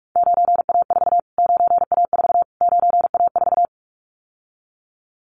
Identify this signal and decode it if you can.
Morse code, heard as a keyed tone, about 44 wpm, 700 Hz. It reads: 9A4 9A4 9A4
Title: Callsign Prefixes for Top 25 DXCC - Repeated 3x - 45wpm